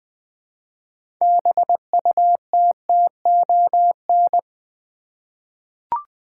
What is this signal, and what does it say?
Morse code, heard as a keyed tone, about 20 wpm, 700 Hz. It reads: BUTTON
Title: 6 Letter Words - First 3 Letters Given - 20wpm